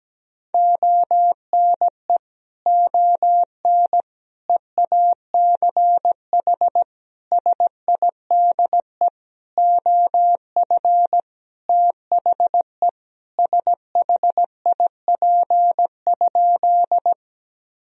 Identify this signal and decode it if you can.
Morse code, heard as a keyed tone, about 17 wpm, 700 Hz. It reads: ONE ON EACH SIDE OF THE SHIP?
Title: Sentences from Top 1000 Words - Review Part 2 - 17wpm